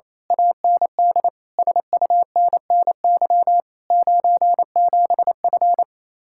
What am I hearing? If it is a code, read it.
Morse code, heard as a keyed tone, about 27 wpm, 700 Hz. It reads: AND SUNNY 97F
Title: Speed-Racing Common QSO Phrases - 28wpm